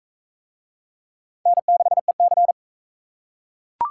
Morse code, heard as a keyed tone, about 39 wpm, 700 Hz. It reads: N6EC